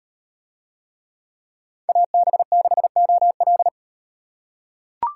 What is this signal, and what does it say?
Morse code, heard as a keyed tone, about 38 wpm, 700 Hz. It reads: AB6OL